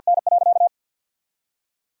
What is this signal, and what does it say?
Morse code, heard as a keyed tone, about 50 wpm, 700 Hz. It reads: N.
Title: Sentences from Top 1000 Words - Review Part 3 - 50wpm